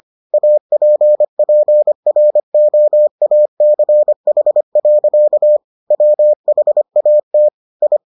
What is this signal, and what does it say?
Morse code, heard as a keyed tone, about 25 wpm, 600 Hz. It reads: APPROACH. WHAT I